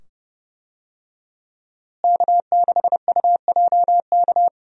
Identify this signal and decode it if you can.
Morse code, heard as a keyed tone, about 30 wpm, 700 Hz. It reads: K6UJK
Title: USA Call Signs - 5 Characters - Repeated 2x - 30wpm